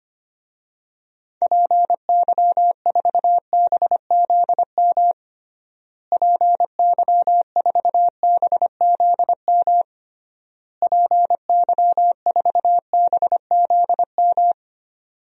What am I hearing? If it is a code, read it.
Morse code, heard as a keyed tone, about 25 wpm, 700 Hz. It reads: PY4BZM PY4BZM PY4BZM
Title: DXCC-100 Call Signs - 6 Characters - Repeated 3x - 25wpm